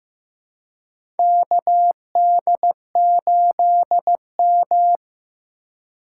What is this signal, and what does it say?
Morse code, heard as a keyed tone, about 15 wpm, 700 Hz. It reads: KD8M
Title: USA Call Signs - 4 Characters - 3x Character Spacing - 15wpm